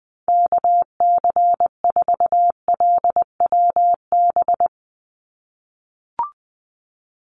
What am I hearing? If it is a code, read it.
Morse code, heard as a keyed tone, about 20 wpm, 700 Hz. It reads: KC4LWB
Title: USA Call Signs 20wpm